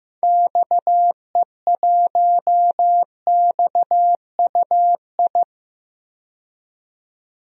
Morse code, heard as a keyed tone, about 15 wpm, 700 Hz. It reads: XE1XUI
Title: DXCC-100 Call Signs - 6 Characters - 15wpm